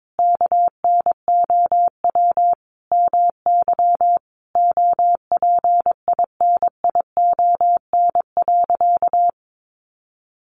Morse code, heard as a keyed tone, about 22 wpm, 700 Hz. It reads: KNOW MY OPINION.
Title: Sentences from Top 1200 Words 22wpm